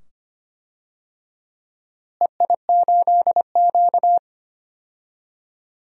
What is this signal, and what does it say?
Morse code, heard as a keyed tone, about 25 wpm, 700 Hz. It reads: EI8Q
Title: DXCC-100 Call Signs - 4 Characters - Repeated 2x - 25wpm